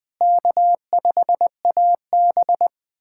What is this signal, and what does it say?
Morse code, heard as a keyed tone, about 20 wpm, 700 Hz. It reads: K5AB
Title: USA Callsign DE Callsign 20wpm